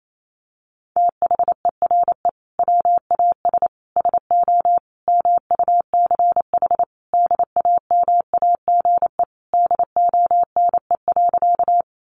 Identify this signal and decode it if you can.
Morse code, heard as a keyed tone, about 28 wpm, 700 Hz. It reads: THERE WAS SO MUCH DAMAGE DONE.
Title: Sentences from Top 1100 Words 28wpm